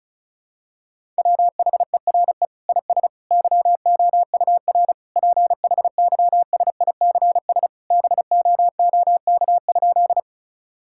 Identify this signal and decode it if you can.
Morse code, heard as a keyed tone, about 36 wpm, 700 Hz. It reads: WHERE IS YOUR PHYSICS BOOK?
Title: Sentences from Top 1000 Words - Review Part 5 - 35wpm